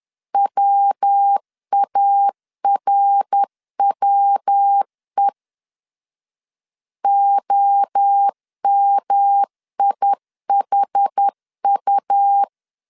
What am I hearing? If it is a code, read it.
Morse code, heard as a keyed tone, about 10 wpm, 800 Hz. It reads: WARWE OMIHU
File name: CW / DET / 1 3 1500C S C 10wpm 3335